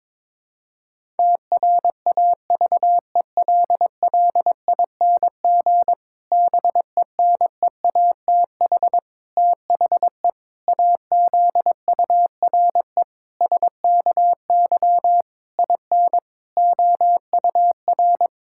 Morse code, heard as a keyed tone, about 22 wpm, 700 Hz. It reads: TRAVELLING BENEATH THE AZURE SKY IN OUR
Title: Pangram Challenge v2 - 22wpm